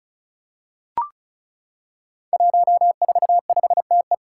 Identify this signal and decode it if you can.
Morse code, heard as a keyed tone, about 35 wpm, 700 Hz. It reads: 145TE
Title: Five Challenging Letter-Number Combinations 35wpm